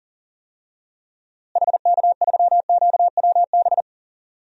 Morse code, heard as a keyed tone, about 37 wpm, 700 Hz. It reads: HK3QWB
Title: DXCC-100 Call Signs - 6 Characters - Repeated 2x - 40wpm